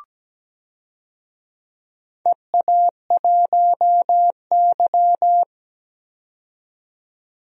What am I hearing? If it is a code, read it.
Morse code, heard as a keyed tone, about 17 wpm, 700 Hz. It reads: EA1Y
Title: DXCC-100 Call Signs - 4 Characters - 17wpm